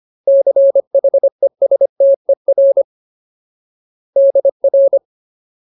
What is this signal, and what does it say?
Morse code, heard as a keyed tone, about 25 wpm, 550 Hz. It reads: CHESTER DR
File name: Ultimate Word List / 25WPM Ultimate